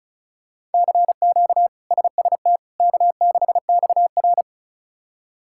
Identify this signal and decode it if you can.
Morse code, heard as a keyed tone, about 36 wpm, 700 Hz. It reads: CQ SST K6XR
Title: K1USN SST QSO Practice - 35wpm